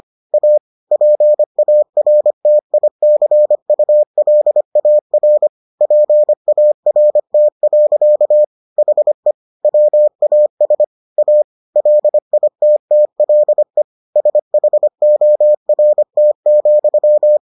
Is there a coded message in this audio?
Yes: Morse code from 0.3 to 17.4 s, heard as a keyed tone, about 25 wpm, 600 Hz. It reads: A PARTICULAR PART. HE WAS A LITTLE SHORT,